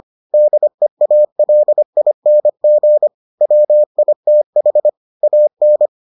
Morse code, heard as a keyed tone, about 25 wpm, 600 Hz. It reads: DEALING WITH AN